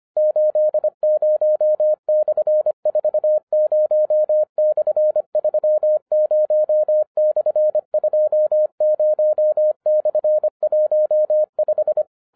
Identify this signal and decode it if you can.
Morse code, heard as a keyed tone, about 25 wpm, 610 Hz. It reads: 80/40/30/20/15
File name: 2011-07-mp3 / n24 QRP OTAKU 25wpm